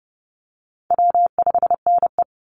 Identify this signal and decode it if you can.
Morse code, heard as a keyed tone, about 30 wpm, 700 Hz. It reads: W5NE
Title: Easy Call Signs - 30wpm